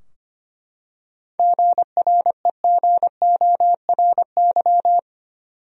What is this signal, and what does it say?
Morse code, heard as a keyed tone, about 25 wpm, 700 Hz. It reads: GREGORY
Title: USA Common Names - 25wpm